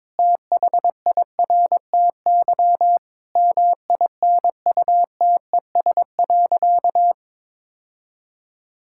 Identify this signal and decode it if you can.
Morse code, heard as a keyed tone, about 22 wpm, 700 Hz. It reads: THIRTY MINUTES.
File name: Difficult Sentences with Hints - 22wpm